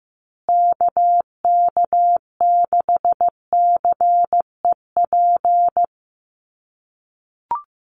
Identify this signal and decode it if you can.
Morse code, heard as a keyed tone, about 15 wpm, 700 Hz. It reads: KK6CEP
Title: USA Call Signs Repeated 3x 15wpm